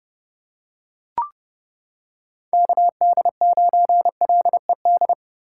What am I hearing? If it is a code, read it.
Morse code, heard as a keyed tone, about 29 wpm, 700 Hz. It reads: KD9LED